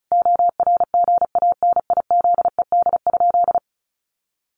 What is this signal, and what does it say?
Morse code, heard as a keyed tone, about 34 wpm, 700 Hz. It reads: ORGANIZED?
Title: Sentences from Top 1300 Words 35wpm